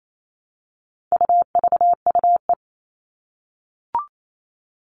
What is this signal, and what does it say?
Morse code, heard as a keyed tone, about 28 wpm, 700 Hz. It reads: UVUE
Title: All Cut Numbers - 4-Digit 28wpm